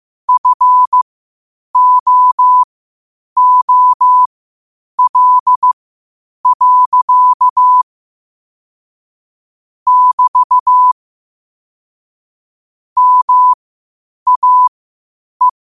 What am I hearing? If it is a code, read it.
Morse code, heard as a keyed tone, about 15 wpm, spaced as for 8 wpm, 1000 Hz. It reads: FOOL. <BT> MAE
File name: qotdfiles / 8wpm